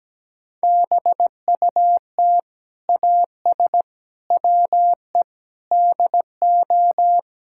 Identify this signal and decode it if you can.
Morse code, heard as a keyed tone, about 17 wpm, 700 Hz. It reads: BUT AS WE DO
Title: Sentences from Top 1600 Words 17wpm